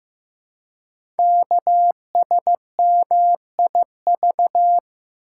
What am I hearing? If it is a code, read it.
Morse code, heard as a keyed tone, about 15 wpm, 700 Hz. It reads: KSMIV
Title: Code Groups - 5 Letters - 15wpm